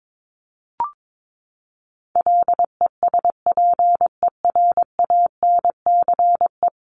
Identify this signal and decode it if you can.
Morse code, heard as a keyed tone, about 22 wpm, 700 Hz. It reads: LESPERANCE
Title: A Sound Of Thunder 22wpm